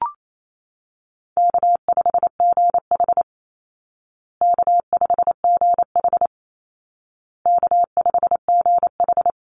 Morse code, heard as a keyed tone, about 28 wpm, 700 Hz. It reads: K5GH K5GH K5GH
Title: Call Signs Repeated 2x - 28wpm